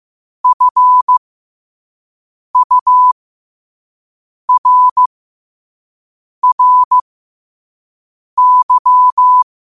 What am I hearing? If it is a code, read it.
Morse code, heard as a keyed tone, about 15 wpm, spaced as for 5 wpm, 1000 Hz. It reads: FURRY
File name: qotdfiles / 5wpm